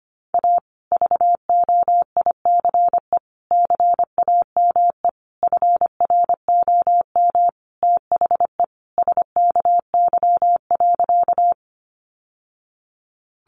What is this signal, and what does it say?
Morse code, heard as a keyed tone, about 25 wpm, 700 Hz. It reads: A VOICE CAME FROM THE SKY.
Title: Sentences from Top 1200 Words 25wpm